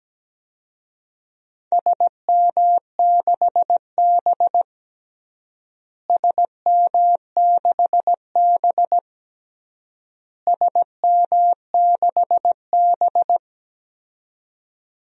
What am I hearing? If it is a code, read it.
Morse code, heard as a keyed tone, about 17 wpm, 700 Hz. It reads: SM6B SM6B SM6B
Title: DXCC-100 Call Signs - 4 Characters - Repeated 3x - 17wpm